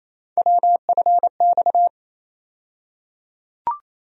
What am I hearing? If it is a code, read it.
Morse code, heard as a keyed tone, about 28 wpm, 700 Hz. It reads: WFX
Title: USA Call Sign Postfixes - 3 Letters 28wpm